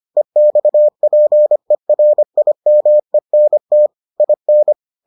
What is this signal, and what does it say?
Morse code, heard as a keyed tone, about 25 wpm, 600 Hz. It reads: EXPERIMENT IN